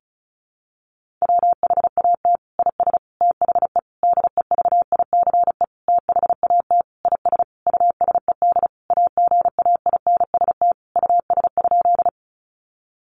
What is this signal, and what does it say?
Morse code, heard as a keyed tone, about 34 wpm, 700 Hz. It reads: WHAT IS THE DEVICE THAT IS USED AGAINST US?
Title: Sentences from Top 1500 Words 35wpm